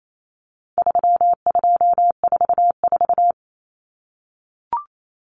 Morse code, heard as a keyed tone, about 28 wpm, 700 Hz. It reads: 3244
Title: Common Cut Numbers - 4-Digit 28wpm